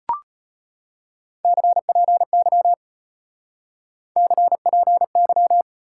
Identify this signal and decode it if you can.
Morse code, heard as a keyed tone, about 33 wpm, 700 Hz. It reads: CPY CPY